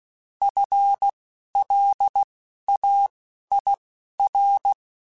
Morse code, heard as a keyed tone, about 16 wpm, spaced as for 11 wpm, 800 Hz. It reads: FLAIR
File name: Alphabetical / Alphabetical mp3s / Fs @10